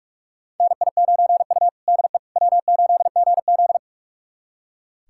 Morse code, heard as a keyed tone, about 44 wpm, 690 Hz. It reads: NI9U DE W8GZ
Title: USA Callsign DE Callsign - 4 Characters - 45wpm